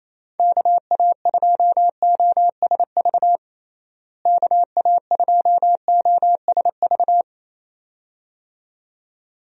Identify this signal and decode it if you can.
Morse code, heard as a keyed tone, about 28 wpm, 700 Hz. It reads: KA2OSV KA2OSV
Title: USA Call Signs - 6 Characters - Repeated 2x - 28wpm